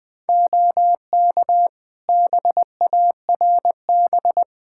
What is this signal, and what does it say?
Morse code, heard as a keyed tone, about 21 wpm, 700 Hz. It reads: OK BARB
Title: Practice QSO - Exchange 2 - Info - 20wpm